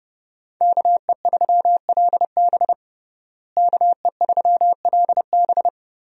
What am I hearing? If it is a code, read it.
Morse code, heard as a keyed tone, about 28 wpm, 700 Hz. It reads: KE3LB KE3LB
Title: USA Call Signs - 5 Characters - Repeated 3x - 30wpm